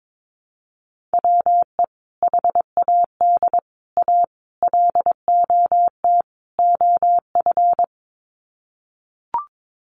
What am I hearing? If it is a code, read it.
Morse code, heard as a keyed tone, about 22 wpm, 700 Hz. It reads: WE HAD A LOT OF
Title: Sets of 5 Words Encore 22wpm